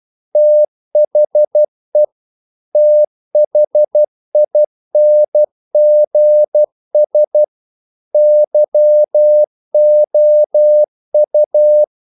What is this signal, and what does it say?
Morse code, heard as a keyed tone, about 12 wpm, 600 Hz. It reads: THE THINGS YOU
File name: CW / Practice / If12